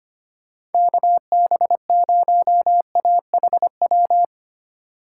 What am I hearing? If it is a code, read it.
Morse code, heard as a keyed tone, about 32 wpm, 700 Hz. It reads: KB0AHW